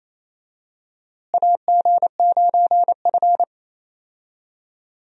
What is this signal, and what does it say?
Morse code, heard as a keyed tone, about 28 wpm, 700 Hz. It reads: AG9F
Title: USA Call Signs - 4 Characters - Repeated 3x - 28wpm